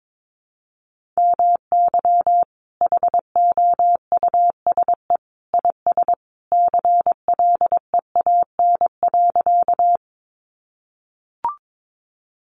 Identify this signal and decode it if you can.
Morse code, heard as a keyed tone, about 22 wpm, 700 Hz. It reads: MY HOUSE IS CLEAN.
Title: Sentences from Top 1100 Words 22wpm